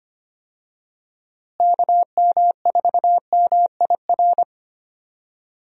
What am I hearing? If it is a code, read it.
Morse code, heard as a keyed tone, about 25 wpm, 700 Hz. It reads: KM4MIR
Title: USA Call Signs - 6 Characters - Repeated 2x - 25wpm